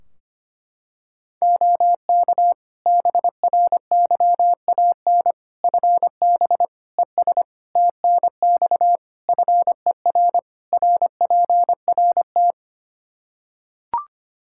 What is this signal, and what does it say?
Morse code, heard as a keyed tone, about 25 wpm, 700 Hz. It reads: OK BRYAN FB ES TNX FER RPRT
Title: Practice QSO - Exchange 2 - Info - 25wpm